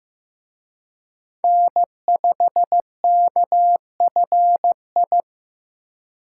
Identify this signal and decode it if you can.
Morse code, heard as a keyed tone, about 18 wpm, 700 Hz. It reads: N5KFI
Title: Speed Racing - USA 1x3 Call Signs 15wpm